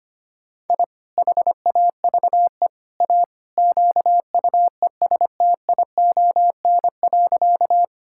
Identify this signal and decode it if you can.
Morse code, heard as a keyed tone, about 25 wpm, 700 Hz. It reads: I HAVE A QUESTION.
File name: practice / 25wpm / Sentences with 3-Word Phrases - 25wpm